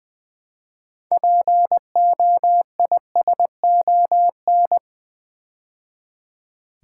Morse code, heard as a keyed tone, about 20 wpm, 700 Hz. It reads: POISON
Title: Build A Word - 2 Letters At A Time - 6-Letter Words - 20wpm